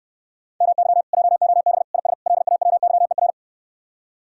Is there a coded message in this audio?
Yes: Morse code from 0.6 to 3.3 s, heard as a keyed tone, about 69 wpm, 700 Hz. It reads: MY JOB IS LAWYER